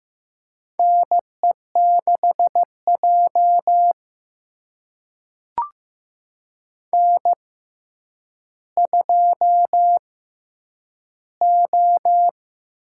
Morse code, heard as a keyed tone, about 15 wpm, spaced as for 9 wpm, 700 Hz. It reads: NE6J N2O
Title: USA Call Signs - 4 Characters - 6x Character Spacing - 15wpm